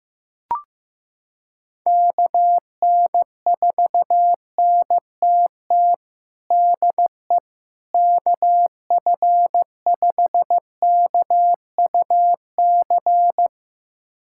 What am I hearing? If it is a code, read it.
Morse code, heard as a keyed tone, about 15 wpm, 700 Hz. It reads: KN4NTT DE KF5KUC
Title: USA Callsign DE Callsign 15wpm